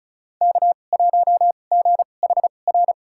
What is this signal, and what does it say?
Morse code, heard as a keyed tone, about 34 wpm, spaced as for 25 wpm, 700 Hz. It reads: K1GHR
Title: USA Call Signs - 5 Characters - 2x Character Spacing - 35wpm